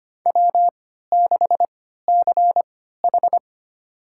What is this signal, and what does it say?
Morse code, heard as a keyed tone, about 25 wpm, spaced as for 14 wpm, 700 Hz. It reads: W6CH
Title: USA Call Signs - 3x Character Spacing - 25wpm